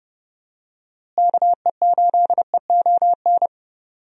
Morse code, heard as a keyed tone, about 32 wpm, 700 Hz. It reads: KE8EON
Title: USA Call Signs - 6 Characters - Repeated 2x - 30wpm